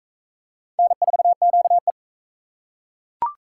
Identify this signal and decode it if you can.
Morse code, heard as a keyed tone, about 39 wpm, 700 Hz. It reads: N4QE